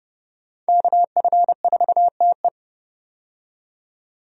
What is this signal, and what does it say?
Morse code, heard as a keyed tone, about 30 wpm, 700 Hz. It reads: KF4TE
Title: USA Call Signs - 5 Characters - 30wpm